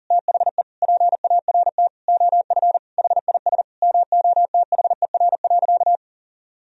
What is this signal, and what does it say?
Morse code, heard as a keyed tone, about 40 wpm, 700 Hz. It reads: THE PART OF HIS MOTHER.
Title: Sentences from Top 1000 Words - Review Part 4 - 40wpm